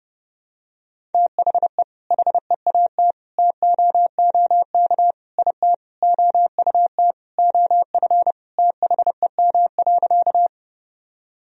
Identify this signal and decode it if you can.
Morse code, heard as a keyed tone, about 30 wpm, 700 Hz. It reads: THE HEAT TOOK IT OUT OF THEM.
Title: Sentences from Top 1000 Words - Review Part 3 - 30wpm